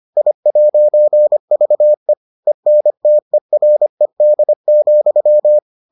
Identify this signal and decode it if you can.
Morse code, heard as a keyed tone, about 25 wpm, 600 Hz. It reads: I'VE ENTERED,